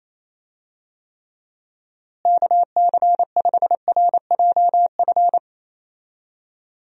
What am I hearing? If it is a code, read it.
Morse code, heard as a keyed tone, about 28 wpm, 700 Hz. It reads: KC5RJF
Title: USA Call Signs - 6 Characters - Repeated 2x - 28wpm